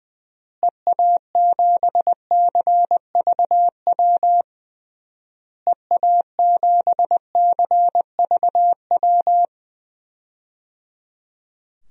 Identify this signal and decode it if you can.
Morse code, heard as a keyed tone, about 20 wpm, 700 Hz. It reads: EA7CVW EA7CVW
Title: DXCC-100 Call Signs - 6 Characters - Repeated 3x - 20wpm